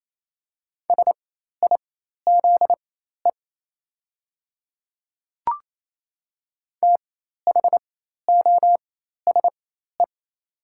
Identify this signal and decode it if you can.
Morse code, heard as a keyed tone, about 28 wpm, spaced as for 13 wpm, 700 Hz. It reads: SIZE THOSE
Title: 4x Character Spacing - Top 500 Words - 28wpm